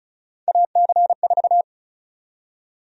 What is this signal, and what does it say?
Morse code, heard as a keyed tone, about 36 wpm, 700 Hz. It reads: AC4
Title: USA Call Sign Prefixes - 35wpm